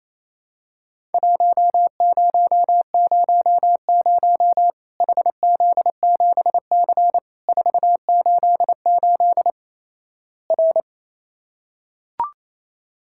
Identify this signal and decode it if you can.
Morse code, heard as a keyed tone, about 28 wpm, 700 Hz. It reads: 1000 HZ7C 488 R
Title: Worked All Europe DX Contest - QTC Exchange - 28wpm